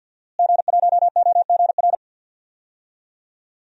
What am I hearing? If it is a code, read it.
Morse code, heard as a keyed tone, about 50 wpm, 700 Hz. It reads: G1OGR
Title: DXCC-100 Call Signs - 5 Characters - Repeated 2x - 50wpm